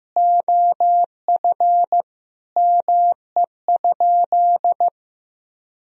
Morse code, heard as a keyed tone, about 15 wpm, 700 Hz. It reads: OF ME?
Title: Sentences from Top 1000 Words - Review Part 4 - 15wpm